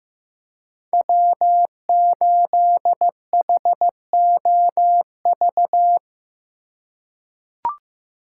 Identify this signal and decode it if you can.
Morse code, heard as a keyed tone, about 15 wpm, 700 Hz. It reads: W8HOV
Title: USA Call Signs - 3x Character Spacing - 15wpm